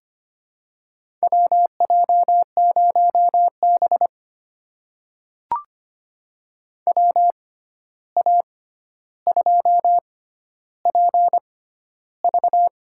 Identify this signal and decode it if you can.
Morse code, heard as a keyed tone, about 25 wpm, spaced as for 12 wpm, 700 Hz. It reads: WJ0B WA2PV
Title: USA Call Signs - 6x Character Spacing - 25wpm